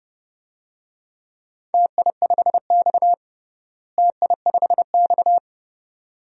Morse code, heard as a keyed tone, about 30 wpm, 700 Hz. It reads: TI5X TI5X
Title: DXCC-100 Call Signs - 4 Characters - Repeated 2x - 30wpm